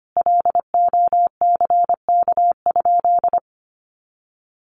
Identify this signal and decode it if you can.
Morse code, heard as a keyed tone, about 25 wpm, 700 Hz. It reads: LOCK?
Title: Sentences from Top 1300 Words 25wpm